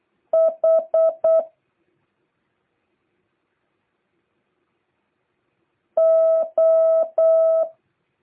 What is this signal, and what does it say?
Morse code, heard as a keyed tone, about 8 wpm, 650 Hz. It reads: H O